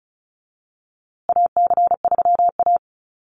Morse code, heard as a keyed tone, about 34 wpm, 700 Hz. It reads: AC3A
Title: USA Call Signs Repeated 3x 35wpm